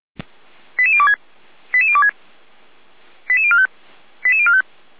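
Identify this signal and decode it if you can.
Five-tone selective call (ZVEI), heard as quick runs of five tones, tones of about 70 ms. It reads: 88026 88026 88045 88045
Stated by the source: ZVEI five-tone selective call received by SDR.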